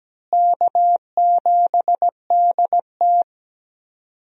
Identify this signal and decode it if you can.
Morse code, heard as a keyed tone, about 17 wpm, 700 Hz. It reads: K7DT
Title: USA Call Signs - 4 Characters - Repeated 2x - 17wpm